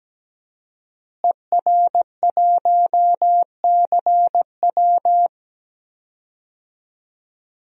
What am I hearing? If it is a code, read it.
Morse code, heard as a keyed tone, about 17 wpm, 700 Hz. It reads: ER1CW
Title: DXCC-100 Call Signs - 5 Characters - Repeated 3x - 17wpm